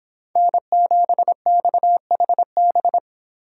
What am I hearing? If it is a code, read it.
Morse code, heard as a keyed tone, about 24 wpm, 700 Hz. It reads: N7XHB